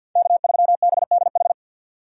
Morse code, heard as a keyed tone, about 51 wpm, 700 Hz. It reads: K3BDH
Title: USA Call Signs - 3x Character Spacing - 50wpm